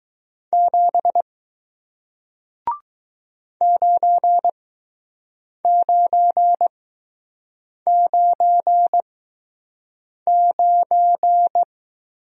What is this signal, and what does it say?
Morse code, heard as a keyed tone, about 20 wpm, spaced as for 6 wpm, 700 Hz. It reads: 7 9999